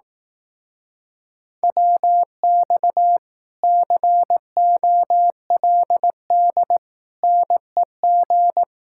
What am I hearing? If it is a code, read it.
Morse code, heard as a keyed tone, about 18 wpm, 700 Hz. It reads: WX COLD NEG
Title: Speed-Racing Common QSO Phrases Encore - 15wpm